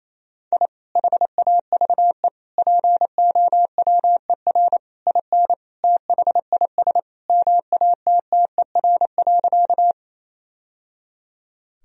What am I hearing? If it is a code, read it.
Morse code, heard as a keyed tone, about 28 wpm, 700 Hz. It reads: I HAVE POWER IN THIS MATTER.
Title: Sentences from Top 1000 Words - Review Part 2 - 28wpm